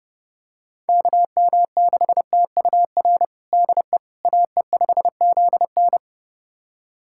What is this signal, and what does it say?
Morse code, heard as a keyed tone, about 30 wpm, 700 Hz. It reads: KM6TUR DE AE5ZN
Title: USA Callsign DE Callsign 30wpm